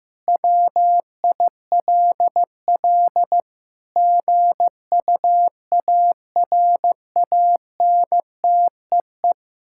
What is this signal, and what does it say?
Morse code, heard as a keyed tone, about 15 wpm, 700 Hz. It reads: WILL GUARANTEE
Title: Sentences from Top 1600 Words 15wpm